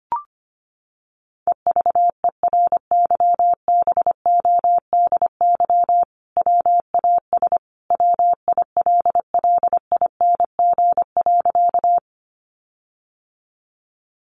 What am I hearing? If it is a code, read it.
Morse code, heard as a keyed tone, about 25 wpm, 700 Hz. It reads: EVERYBODY WAS WILLING.
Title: Sentences from Top 1400 Words 25wpm